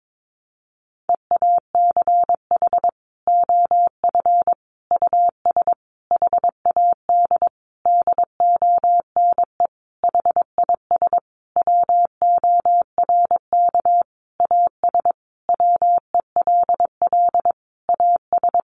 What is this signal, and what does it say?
Morse code, heard as a keyed tone, about 22 wpm, 700 Hz. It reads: EACH OF US HAD DONE HIS WORK AS WELL AS
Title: Sentences from Top 500 Words 22wpm